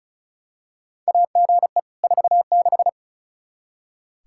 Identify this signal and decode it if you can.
Morse code, heard as a keyed tone, about 36 wpm, 700 Hz. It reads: AGE 46
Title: K1USN SST QSO Practice 35wpm